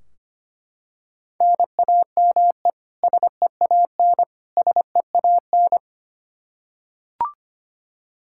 Morse code, heard as a keyed tone, about 25 wpm, 700 Hz. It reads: NAME SEAN SEAN
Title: Practice QSO - Exchange 1 - Beginning - 25wpm